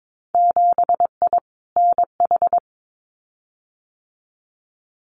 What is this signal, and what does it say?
Morse code, heard as a keyed tone, about 22 wpm, 700 Hz. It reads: 7I NH
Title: Winter Field Day Exchanges Encore 22wpm